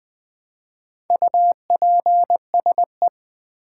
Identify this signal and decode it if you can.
Morse code, heard as a keyed tone, about 20 wpm, 700 Hz. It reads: UPSE
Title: Build A Word - 1 Letter At A Time - 5-Letter Words - 20wpm